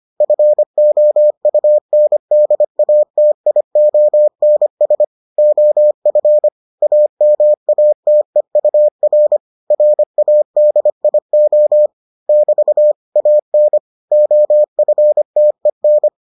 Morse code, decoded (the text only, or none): FOUNDATIONS OF AMATEUR RADIO <BT> AN OFTEN